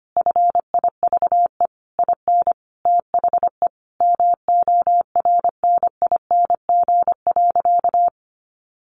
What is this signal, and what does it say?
Morse code, heard as a keyed tone, about 25 wpm, 700 Hz. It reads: FIVE IN THE MORNING.
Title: Sentences from Top 1300 Words 25wpm